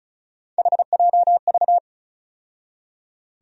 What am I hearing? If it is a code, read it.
Morse code, heard as a keyed tone, about 36 wpm, 700 Hz. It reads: HJV
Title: USA Call Sign Postfixes - 3 Letters 35wpm